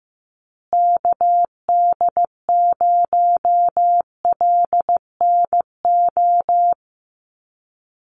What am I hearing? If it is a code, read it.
Morse code, heard as a keyed tone, about 15 wpm, 700 Hz. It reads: KD0LNO
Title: USA Call Signs Repeated 3x 15wpm